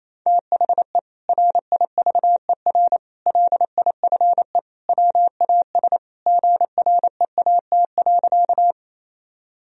Morse code, heard as a keyed tone, about 28 wpm, 700 Hz. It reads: THE RIVER LIFE WAS GREAT.
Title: Sentences from Top 1000 Words - Review Part 1 - 28wpm